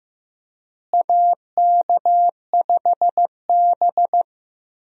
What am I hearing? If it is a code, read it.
Morse code, heard as a keyed tone, about 15 wpm, 700 Hz. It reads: AK5B
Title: USA Call Signs - 4 Characters - Repeated 2x - 15wpm